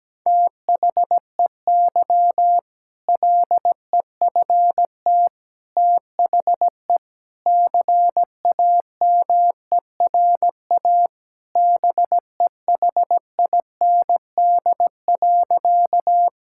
Morse code, heard as a keyed tone, about 17 wpm, 700 Hz. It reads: THEY LEFT THE CAMERA BEHIND.
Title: Sentences from Top 1000 Words - Review Part 5 - 17wpm